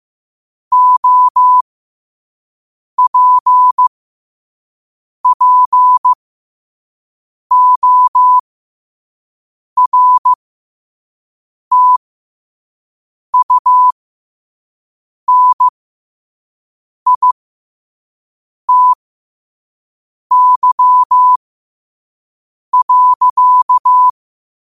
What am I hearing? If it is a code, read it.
Morse code, heard as a keyed tone, about 15 wpm, spaced as for 5 wpm, 1000 Hz. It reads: OPPORTUNITY.
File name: qotdfiles / 5wpm